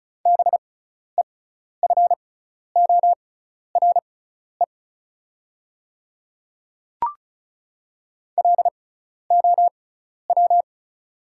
Morse code, heard as a keyed tone, about 34 wpm, spaced as for 12 wpm, 690 Hz. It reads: BEFORE LOW
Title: 6x Character Spacing - Top 200 Words - 35wpm